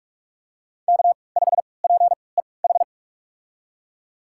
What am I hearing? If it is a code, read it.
Morse code, heard as a keyed tone, about 46 wpm, spaced as for 26 wpm, 700 Hz. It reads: K5PEH
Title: USA Call Signs - 5 Characters - 3x Character Spacing - 45wpm